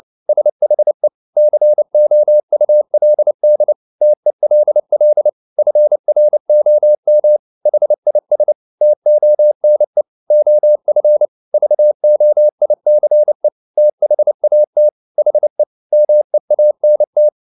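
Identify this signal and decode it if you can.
Morse code, heard as a keyed tone, about 29 wpm, 600 Hz. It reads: SHE COULD TELL FROM HIS TONE OF VOICE THAT HE MEANT
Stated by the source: Morse code at 29 wpm, sending Ken Follett's novel Never in English, from CW med Gnister.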